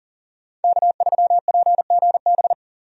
Morse code, heard as a keyed tone, about 40 wpm, 700 Hz. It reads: K3PGB
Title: USA Call Signs - 5 Characters - 40wpm